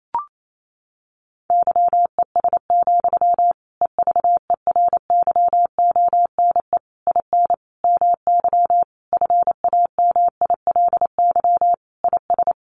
Morse code, heard as a keyed tone, about 28 wpm, 700 Hz. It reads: YES, EVERYONE IN MY FAMILY IS
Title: Sentences from Top 800 Words 28wpm